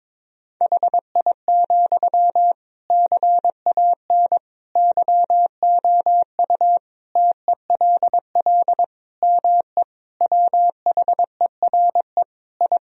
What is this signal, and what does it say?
Morse code, heard as a keyed tone, about 21 wpm, 700 Hz. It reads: HI, CAN YOU TELL ME WHERE I
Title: Sentences from Top 1000 Words - Review Part 3 - 22wpm